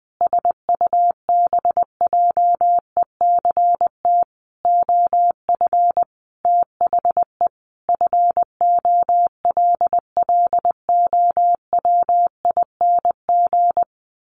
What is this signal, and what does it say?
Morse code, heard as a keyed tone, about 19 wpm, 700 Hz. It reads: SUBJECT OF THE FOLLOWING
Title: Sentences from Top 900 Words 20wpm